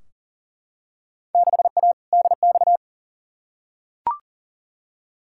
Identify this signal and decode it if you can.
Morse code, heard as a keyed tone, about 40 wpm, 700 Hz. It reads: 6A DX
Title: Field Day Exchanges Encore - 40wpm